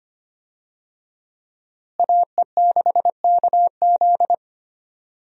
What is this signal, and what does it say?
Morse code, heard as a keyed tone, about 25 wpm, 700 Hz. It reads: AE6KZ
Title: USA Call Signs - 5 Characters - Repeated 2x - 25wpm